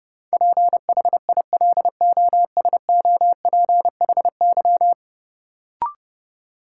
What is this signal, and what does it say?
Morse code, heard as a keyed tone, about 30 wpm, 700 Hz. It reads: PHILOSOPHY
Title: Sentences from Top 900 Words - New Word Drill - 30wpm